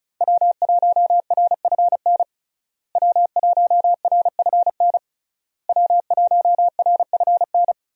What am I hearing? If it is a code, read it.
Morse code, heard as a keyed tone, about 34 wpm, 700 Hz. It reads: W1RFN W1RFN W1RFN
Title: USA Call Signs - 5 Characters - Repeated 3x - 35wpm